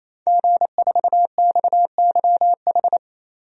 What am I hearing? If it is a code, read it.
Morse code, heard as a keyed tone, about 28 wpm, 700 Hz. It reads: G4XYH
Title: DXCC-100 Call Signs - 5 Characters - 28wpm